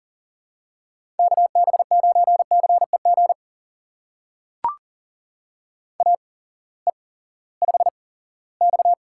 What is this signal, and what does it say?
Morse code, heard as a keyed tone, about 40 wpm, spaced as for 20 wpm, 700 Hz. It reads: KB9CEG AE5X
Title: USA Call Signs - 8x Character Spacing - 40wpm